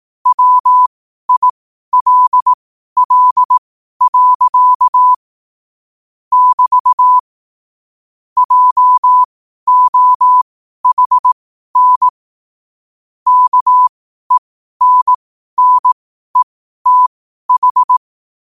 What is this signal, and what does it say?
Morse code, heard as a keyed tone, about 18 wpm, spaced as for 12 wpm, 1000 Hz. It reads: WILL. <BT> JOHN KENNETH
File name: qotdfiles / 12wpm